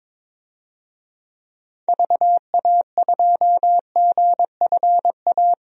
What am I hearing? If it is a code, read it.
Morse code, heard as a keyed tone, about 22 wpm, 700 Hz. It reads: VA2GFA
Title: DXCC-100 Call Signs - 6 Characters - 22wpm